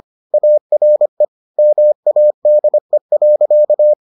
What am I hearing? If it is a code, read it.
Morse code, heard as a keyed tone, about 25 wpm, 600 Hz. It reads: ARE MADE.